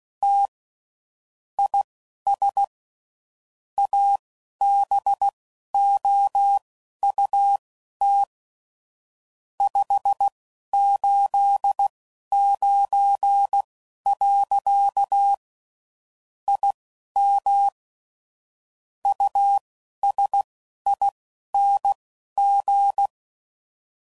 Morse code, heard as a keyed tone, about 16 wpm, spaced as for 11 wpm, 800 Hz. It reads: T IS ABOUT 589. IM USING
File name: CW Code Groups @15WPM / QSOs / WD0FBT Spaced